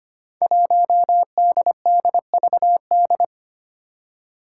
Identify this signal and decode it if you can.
Morse code, heard as a keyed tone, about 25 wpm, 700 Hz. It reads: 1DDVD
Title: Five Challenging Letter-Number Combinations 25wpm